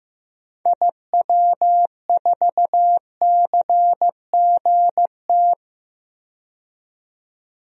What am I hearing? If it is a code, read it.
Morse code, heard as a keyed tone, about 15 wpm, 700 Hz. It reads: IW4CGT
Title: DXCC-100 Call Signs - 6 Characters - 15wpm